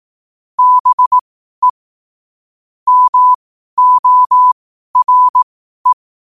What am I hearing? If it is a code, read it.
Morse code, heard as a keyed tone, about 18 wpm, spaced as for 12 wpm, 1000 Hz. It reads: BE MORE
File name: qotdfiles / 12wpm